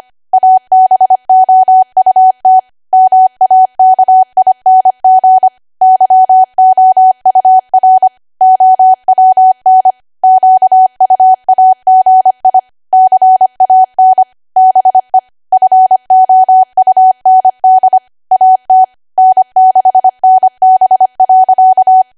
Morse code, heard as a keyed tone, about 25 wpm, 750 Hz. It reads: ABOUT MAKING YOUR OWN QUAGI CAN BE FOUND AT N6NB.
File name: Archive / 25 WPM / 25WPM